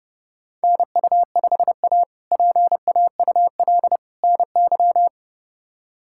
Morse code, heard as a keyed tone, about 30 wpm, 700 Hz. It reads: NU5A PAUL NY
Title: K1USN SST QSO Practice - 30wpm